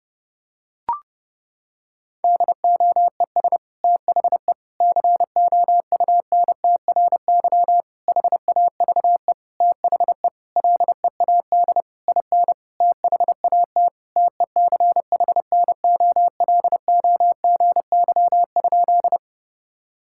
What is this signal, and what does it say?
Morse code, heard as a keyed tone, about 30 wpm, 700 Hz. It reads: DOES THE COUNTRY HAVE THE LEAD IN THAT TECHNOLOGY?
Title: Sentences from Top 1000 Words - Review Part 5 - 30wpm